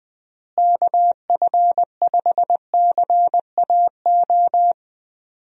Morse code, heard as a keyed tone, about 20 wpm, 700 Hz. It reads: KF5CAO
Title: USA Call Signs - 6 Characters - Repeated 3x - 20wpm